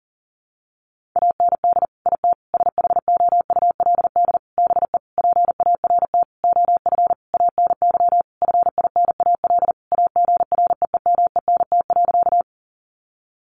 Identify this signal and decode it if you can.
Morse code, heard as a keyed tone, about 40 wpm, 700 Hz. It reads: AND IT SHOULD BE PART OF ANY FINAL AGREEMENT.
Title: Sentences from Top 1500 Words 40wpm